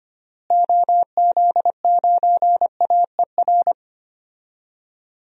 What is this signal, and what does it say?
Morse code, heard as a keyed tone, about 25 wpm, 700 Hz. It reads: OZ9AER
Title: DXCC-100 Call Signs - 6 Characters - Repeated 2x - 25wpm